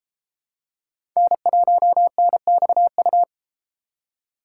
Morse code, heard as a keyed tone, about 30 wpm, 700 Hz. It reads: N1NXU